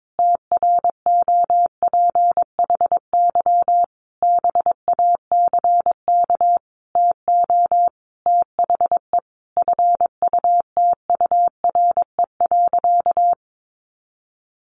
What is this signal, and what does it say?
Morse code, heard as a keyed tone, about 22 wpm, 700 Hz. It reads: TROPHY BACK TO THE FUTURE.
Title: A Sound Of Thunder 22wpm